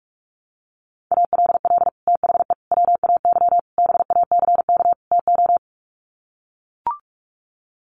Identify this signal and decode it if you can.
Morse code, heard as a keyed tone, about 44 wpm, 700 Hz. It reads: ALL THE WAY BACK TO
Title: Sets of 5 Words Encore 45wpm